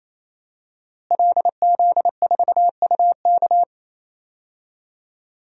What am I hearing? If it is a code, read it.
Morse code, heard as a keyed tone, about 28 wpm, 700 Hz. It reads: LZ4UK
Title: DXCC-100 Call Signs - 5 Characters - Repeated 3x - 28wpm